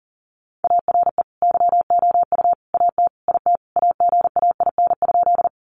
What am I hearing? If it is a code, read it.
Morse code, heard as a keyed tone, about 40 wpm, 700 Hz. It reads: ARE YOU AT IT AGAIN?
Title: Sentences from Top 200 Words 40wpm